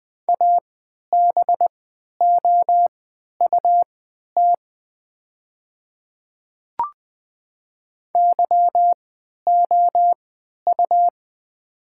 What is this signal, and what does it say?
Morse code, heard as a keyed tone, about 20 wpm, spaced as for 11 wpm, 700 Hz. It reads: ABOUT YOU
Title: 3x Character Spacing - Top 100 Words - 20wpm